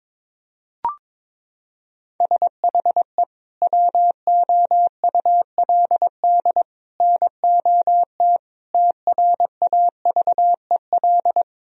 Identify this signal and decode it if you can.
Morse code, heard as a keyed tone, about 22 wpm, 700 Hz. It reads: SHE WOULD NOT TRAVEL
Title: Sentences from Top 1000 Words - Review Part 2 - 22wpm